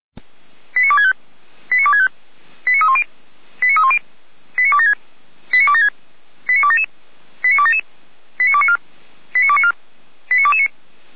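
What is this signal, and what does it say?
Five-tone selective call (ZVEI), heard as quick runs of five tones, tones of about 70 ms. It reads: 89265 89265 89310 89310 89276 89276 89277 89277 89294 89294 89229
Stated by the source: ZVEI five-tone selective call received via SDR, from the feuerwehr-orlamuende.org server (Feuerwehr Orlamünde, fire brigade).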